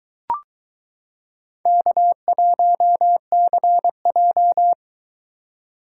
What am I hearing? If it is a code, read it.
Morse code, heard as a keyed tone, about 21 wpm, 700 Hz. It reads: K1CJ